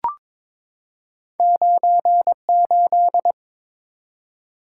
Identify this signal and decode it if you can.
Morse code, heard as a keyed tone, about 22 wpm, 700 Hz. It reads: 98
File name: practice / 22wpm / Two Letter-Number Combinations - 22wpm